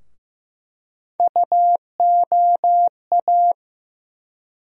Morse code, heard as a keyed tone, about 15 wpm, 700 Hz. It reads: UOA
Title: USA Call Sign Postfixes - 3 Letters 15wpm